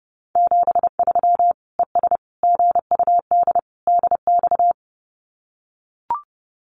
Morse code, heard as a keyed tone, about 30 wpm, 700 Hz. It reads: 73 ES GUD DX
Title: Common QSO Phrases 30wpm